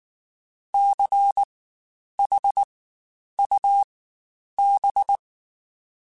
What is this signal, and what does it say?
Morse code, heard as a keyed tone, about 19 wpm, spaced as for 9 wpm, 800 Hz. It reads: CHUB
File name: Spaced Alphas / Spaced Alphas mp3s / Cs sp @18